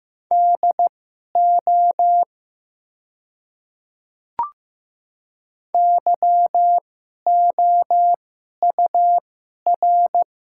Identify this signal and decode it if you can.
Morse code, heard as a keyed tone, about 15 wpm, spaced as for 11 wpm, 700 Hz. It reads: DO YOUR
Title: 2x Character Spacing - Top 100 Words - 15wpm